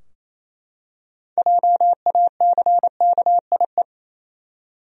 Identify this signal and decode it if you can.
Morse code, heard as a keyed tone, about 28 wpm, 700 Hz. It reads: JACKIE
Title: USA Common Names - 28wpm